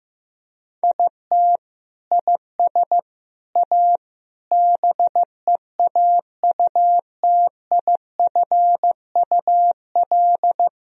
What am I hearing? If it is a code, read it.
Morse code, heard as a keyed tone, about 15 wpm, 700 Hz. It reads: IT IS A BEAUTIFUL
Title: Sentences from Top 1000 Words - Review Part 2 - 15wpm